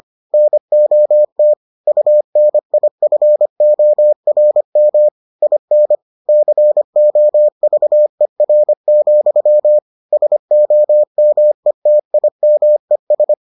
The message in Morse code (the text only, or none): NOT UNIFORM IN COVER, SOMETIMES